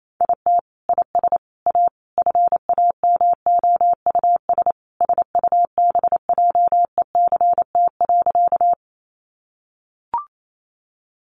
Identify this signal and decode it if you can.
Morse code, heard as a keyed tone, about 28 wpm, 700 Hz. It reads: IT IS A FAMOUS SUBJECT.
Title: Sentences from Top 1500 Words 28wpm